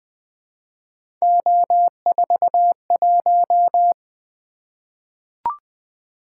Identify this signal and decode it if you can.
Morse code, heard as a keyed tone, about 20 wpm, 700 Hz. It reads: O41
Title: Three Letter-Number Combinations - 20wpm